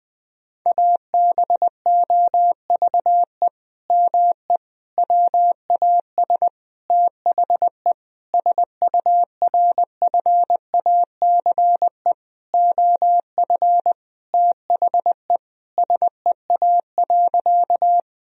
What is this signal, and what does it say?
Morse code, heard as a keyed tone, about 20 wpm, 700 Hz. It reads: ABOVE ME WAS THE SURFACE OF THE SEA.
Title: Sentences from Top 1000 Words - Review Part 3 - 20wpm